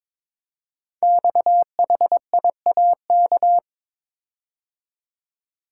Morse code, heard as a keyed tone, about 22 wpm, 700 Hz. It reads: XHIAK
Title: Code Groups - 5 Letter:Numbers - 22wpm